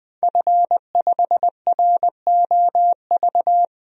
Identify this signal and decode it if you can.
Morse code, heard as a keyed tone, about 21 wpm, 700 Hz. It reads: F5ROV
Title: DXCC-100 Call Signs - 5 Characters - 20wpm